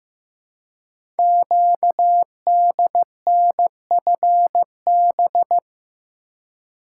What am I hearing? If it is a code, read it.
Morse code, heard as a keyed tone, about 15 wpm, 700 Hz. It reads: QDNFB
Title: Code Groups - 5 Letters - 15wpm